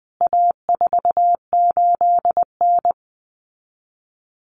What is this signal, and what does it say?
Morse code, heard as a keyed tone, about 20 wpm, 700 Hz. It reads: A48N
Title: CWT Exchange 20wpm